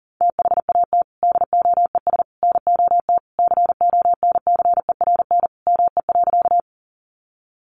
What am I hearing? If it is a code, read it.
Morse code, heard as a keyed tone, about 40 wpm, 700 Hz. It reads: THAT DOES NOT CONCERN ME.
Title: Sentences from Top 1000 Words 40wpm